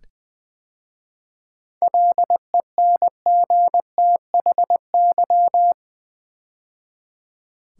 Morse code, heard as a keyed tone, about 20 wpm, 700 Hz. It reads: LENGTHY
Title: 7 Letter Words - First 3 Letters Given - 20wpm